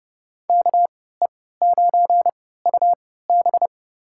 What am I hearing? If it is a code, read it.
Morse code, heard as a keyed tone, about 32 wpm, spaced as for 17 wpm, 700 Hz. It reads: KE9UB
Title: USA Call Signs - 5 Characters - 3x Character Spacing - 30wpm